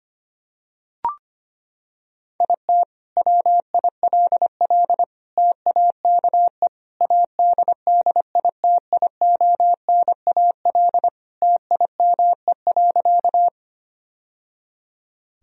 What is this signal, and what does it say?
Morse code, heard as a keyed tone, about 25 wpm, 700 Hz. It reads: IT WILL TAKE ADDITIONAL TIME.
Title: Sentences from Top 1000 Words - Review Part 4 - 25wpm